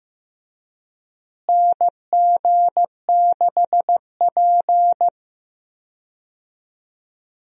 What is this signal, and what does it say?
Morse code, heard as a keyed tone, about 15 wpm, 700 Hz. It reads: NG6P
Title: USA Call Signs - 4 Characters - Repeated 3x - 15wpm